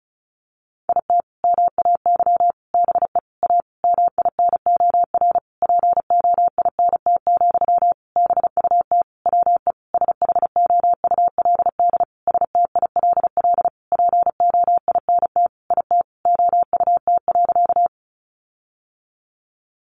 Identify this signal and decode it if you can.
Morse code, heard as a keyed tone, about 34 wpm, 700 Hz. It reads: IT MAY BE A MINOR POINT, BUT WE SHOULD STILL POINT IT OUT.
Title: Sentences from Top 1500 Words 35wpm